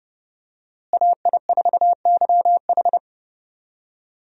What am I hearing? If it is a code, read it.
Morse code, heard as a keyed tone, about 30 wpm, 700 Hz. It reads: AI4YH
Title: USA Call Signs - 5 Characters - Repeated 2x - 30wpm